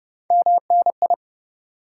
Morse code, heard as a keyed tone, about 37 wpm, 700 Hz. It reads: MNI
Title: Speed-Racing Top 180 QSO Elements 30wpm